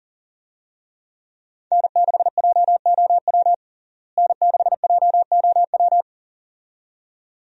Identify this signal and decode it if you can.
Morse code, heard as a keyed tone, about 37 wpm, 700 Hz. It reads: N6JOW N6JOW
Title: USA Call Signs - 5 Characters - Repeated 2x - 40wpm